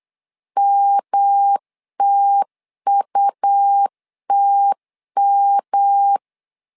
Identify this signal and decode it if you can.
Morse code, heard as a keyed tone, about 8 wpm, 800 Hz. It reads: MTUTM
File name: LEZIONI / 3 MWRO / 3 M 8wpm 2300